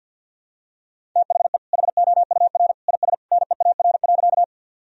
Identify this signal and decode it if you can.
Morse code, heard as a keyed tone, about 52 wpm, 700 Hz. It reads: THE HOUR IS NEAR.
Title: Sentences from Top 1000 Words - Review Part 1 - 50wpm